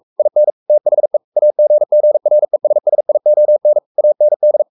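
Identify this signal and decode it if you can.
Morse code, heard as a keyed tone, about 44 wpm, 600 Hz. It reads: IN THE AGGRESSION AND
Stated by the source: Morse code at 43 wpm sending Ken Follett's novel Never in English, from CW med Gnister.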